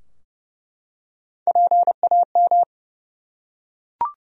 Morse code, heard as a keyed tone, about 30 wpm, 700 Hz. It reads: PAM
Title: USA Common Names - Max 3 Letters - 30wpm